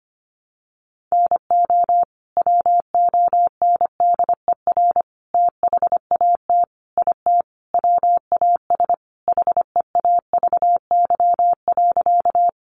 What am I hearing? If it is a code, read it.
Morse code, heard as a keyed tone, about 25 wpm, 700 Hz. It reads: NO WONDER THAT IT WAS HEAVY.
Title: Sentences from Top 1100 Words 25wpm